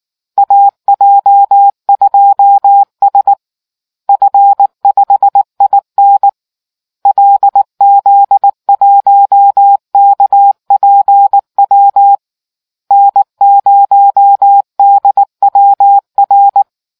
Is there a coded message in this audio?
Yes: Morse code from 0.4 to 16.6 s, heard as a keyed tone, about 19 wpm, 800 Hz. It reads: AJ2S F5IN LZ1KPW N0DWR